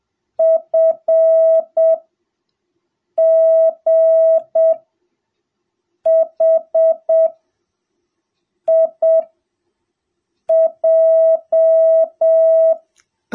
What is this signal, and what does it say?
Morse code, heard as a keyed tone, about 7 wpm, spaced as for 4 wpm, 660 Hz. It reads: FGHIJ